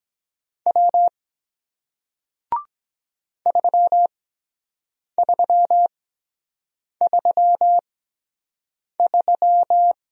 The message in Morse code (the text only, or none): W 3333